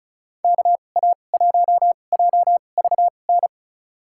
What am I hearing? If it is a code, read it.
Morse code, heard as a keyed tone, about 36 wpm, spaced as for 25 wpm, 700 Hz. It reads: KA1JVN
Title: USA Call Signs - 6 Characters - 2x Character Spacing - 35wpm